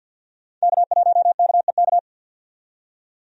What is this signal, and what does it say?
Morse code, heard as a keyed tone, about 50 wpm, 700 Hz. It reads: K1KEK
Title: USA Call Signs - 5 Characters - 50wpm